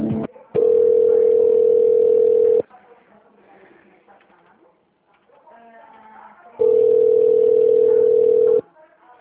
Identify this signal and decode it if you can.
Ringback tone (North America), 440 + 480 Hz, with speech.
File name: out- -2001- - - .9012